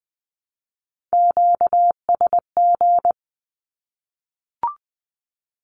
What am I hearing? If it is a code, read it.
Morse code, heard as a keyed tone, about 20 wpm, 700 Hz. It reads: QSG
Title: Q-Codes 20wpm